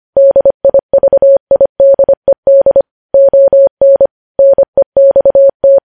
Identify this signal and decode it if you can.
Morse code, heard as a keyed tone, about 25 wpm, 550 Hz. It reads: DIVIDED ON NEXT
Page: news headlines in morse code at 25 wpm